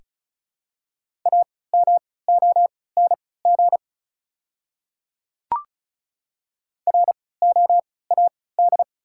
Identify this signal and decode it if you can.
Morse code, heard as a keyed tone, about 36 wpm, spaced as for 20 wpm, 690 Hz. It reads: AMONG ROAD
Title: 3x Character Spacing - Top 500 Words - 35wpm